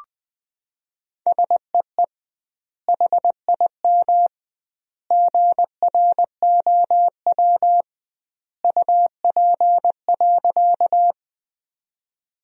Morse code, heard as a keyed tone, about 20 wpm, 700 Hz. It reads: SEE HIM GROW UP.
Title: 2x Word-Spacing - Sentences from Top 300 Words - 20wpm